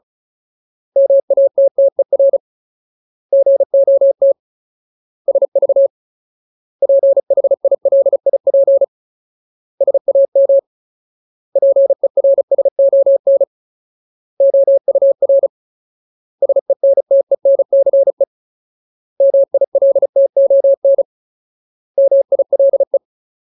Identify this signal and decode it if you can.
Morse code, heard as a keyed tone, about 34 wpm, 550 Hz. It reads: MATTER GOT SV PHILIP SAM PERSON OUR SENTENCE MILTON MILE
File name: Ultimate Word List / 35WPM Ultimate